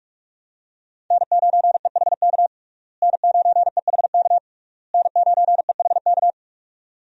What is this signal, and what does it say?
Morse code, heard as a keyed tone, about 46 wpm, 700 Hz. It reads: N9EHK N9EHK N9EHK
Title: USA Call Signs - 5 Characters - Repeated 3x - 45wpm